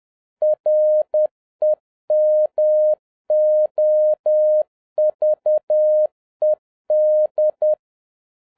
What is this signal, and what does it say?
Morse code, heard as a keyed tone, about 10 wpm, 610 Hz. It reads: REMOVED